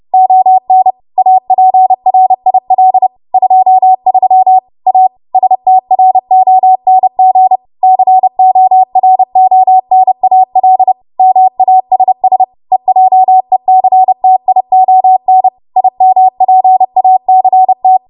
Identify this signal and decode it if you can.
Morse code, heard as a keyed tone, about 30 wpm, 750 Hz. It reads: ON APRIL 23 A STRONG CORONAL MASS EJECTION IMPACT
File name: Morse / 30WPM